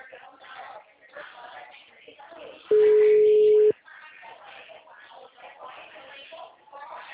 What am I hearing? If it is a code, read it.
Ringback tone (Europe, 425 Hz), with speech.
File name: out- -2014- - - .21588